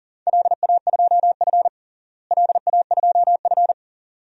Morse code, heard as a keyed tone, about 40 wpm, 700 Hz. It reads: LA2F LA2F
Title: DXCC-100 Call Signs - 4 Characters - Repeated 2x - 40wpm